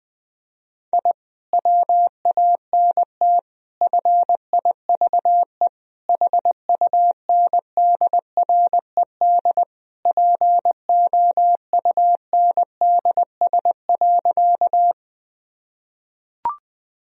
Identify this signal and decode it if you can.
Morse code, heard as a keyed tone, about 21 wpm, 700 Hz. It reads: I WANT FIVE HUNDRED POUNDS.
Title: Sentences from Top 1000 Words - Review Part 2 - 20wpm